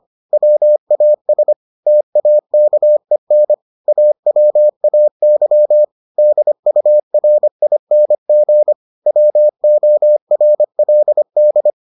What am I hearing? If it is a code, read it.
Morse code, heard as a keyed tone, about 25 wpm, 600 Hz. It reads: WAS TAKEN AWAY DURING WORLD